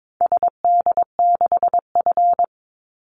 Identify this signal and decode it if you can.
Morse code, heard as a keyed tone, about 22 wpm, 690 Hz. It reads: SD6F
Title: Call Signs Repeated 2x - 22wpm